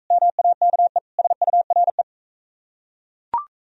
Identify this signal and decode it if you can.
Morse code, heard as a keyed tone, about 41 wpm, 700 Hz. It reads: MAKE SURE